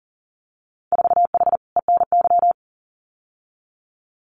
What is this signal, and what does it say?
Morse code, heard as a keyed tone, about 38 wpm, 700 Hz. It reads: 4H ENY
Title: Winter Field Day Exchanges Encore 40wpm